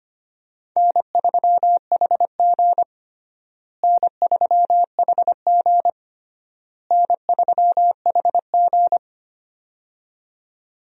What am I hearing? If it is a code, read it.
Morse code, heard as a keyed tone, about 25 wpm, 700 Hz. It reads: N3HG N3HG N3HG
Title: USA Call Signs - 4 Characters - Repeated 3x - 25wpm